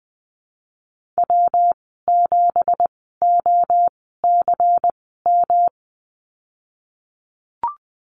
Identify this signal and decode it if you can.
Morse code, heard as a keyed tone, about 20 wpm, spaced as for 15 wpm, 700 Hz. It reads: W7OCM
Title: 179 Lesson 29g - Slash - Call Signs 20wpm